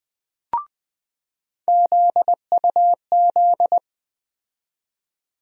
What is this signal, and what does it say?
Morse code, heard as a keyed tone, about 20 wpm, 700 Hz. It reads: ZUZ
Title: USA Call Sign Postfixes - 3 Letters 20wpm